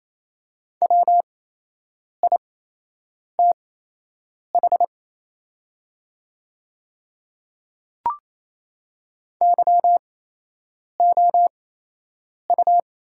Morse code, heard as a keyed tone, about 28 wpm, spaced as for 8 wpm, 700 Hz. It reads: WITH YOU
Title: 8x Character Spacing - Top 100 Words - 28wpm